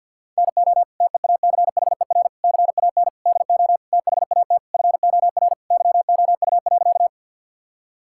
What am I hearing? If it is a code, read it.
Morse code, heard as a keyed tone, about 50 wpm, 700 Hz. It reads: NO TEACHER CAN DO THAT FOR YOU.
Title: Sentences from Top 1000 Words - Review Part 6 - 50wpm